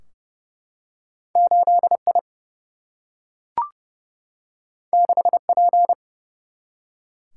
Morse code, heard as a keyed tone, about 30 wpm, 700 Hz. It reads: 8 I 6 P
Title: Single-Letter Single-Number 30wpm